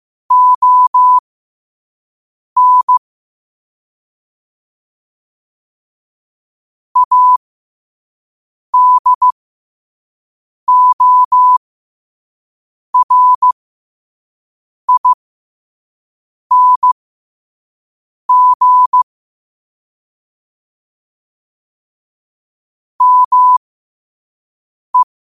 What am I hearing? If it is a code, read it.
Morse code, heard as a keyed tone, about 15 wpm, spaced as for 5 wpm, 1000 Hz. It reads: ON ADORING ME
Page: Quotes for Fri, 15 Aug 2025 in Morse Code at 5 words per minute.